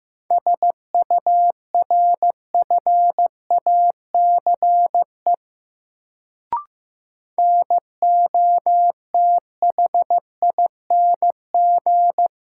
Morse code, heard as a keyed tone, about 15 wpm, 700 Hz. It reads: SURFACE NOTHING
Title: Sets of 4 Words - New Word Drill - 15wpm